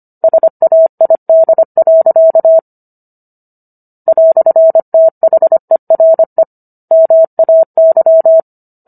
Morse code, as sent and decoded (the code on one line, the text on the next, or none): ... .- .. -.. .-.-.- / .-..-. - .... .-. / -- .- -.--
SAID. "THERE MAY